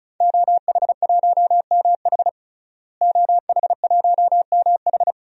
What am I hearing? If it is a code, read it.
Morse code, heard as a keyed tone, about 36 wpm, 700 Hz. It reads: OH1MH OH1MH
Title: DXCC-100 Call Signs - 5 Characters - Repeated 2x - 35wpm